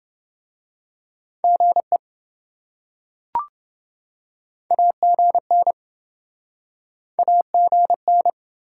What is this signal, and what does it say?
Morse code, heard as a keyed tone, about 27 wpm, 700 Hz. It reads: GE AGN AGN